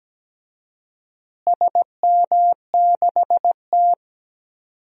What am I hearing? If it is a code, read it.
Morse code, heard as a keyed tone, about 17 wpm, 700 Hz. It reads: SM6T
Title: DXCC-100 Call Signs - 4 Characters - Repeated 3x - 17wpm